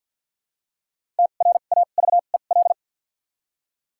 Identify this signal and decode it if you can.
Morse code, heard as a keyed tone, about 54 wpm, spaced as for 37 wpm, 700 Hz. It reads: TRAVEL
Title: 2x Character Spacing - Top 500 Words - 50wpm